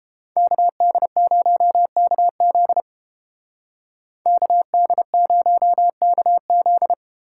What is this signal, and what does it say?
Morse code, heard as a keyed tone, about 31 wpm, 700 Hz. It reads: KD0KZ KD0KZ